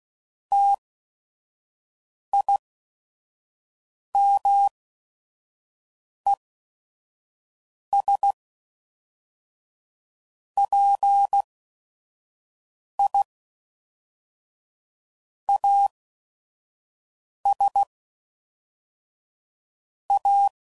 Morse code, heard as a keyed tone, about 16 wpm, spaced as for 5 wpm, 800 Hz. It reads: TIMES PI AS A
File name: CW Code Groups @5WPM / Technical Subjects / XL